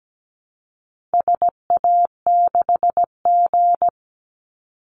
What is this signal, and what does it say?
Morse code, heard as a keyed tone, about 17 wpm, 700 Hz. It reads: SA6G
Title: Call Signs Repeated 3x - 17wpm